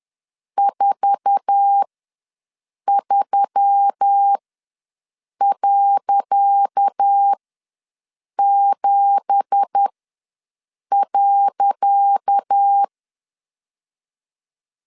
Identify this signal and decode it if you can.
Morse code, heard as a keyed tone, about 10 wpm, spaced as for 6 wpm, 800 Hz. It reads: 43.7.
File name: LEZIONI / 10 SIGNS / 10C DOT ws 10wpm 4213